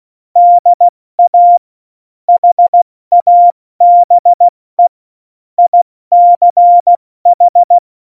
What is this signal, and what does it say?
Morse code, heard as a keyed tone, about 16 wpm, 700 Hz. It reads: DA HABE ICH
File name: Morsekurs / Witze / Witz 29. Oktober 2019, 70 F80